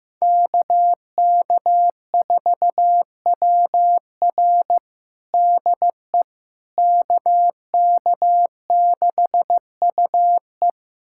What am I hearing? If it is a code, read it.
Morse code, heard as a keyed tone, about 15 wpm, 700 Hz. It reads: KK4WR DE KK6UE
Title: USA Callsign DE Callsign - 5 Characters - 15wpm